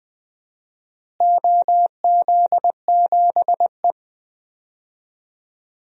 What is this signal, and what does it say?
Morse code, heard as a keyed tone, about 20 wpm, 700 Hz. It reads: OZ7E
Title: DXCC-100 Call Signs - 4 Characters - Repeated 3x - 20wpm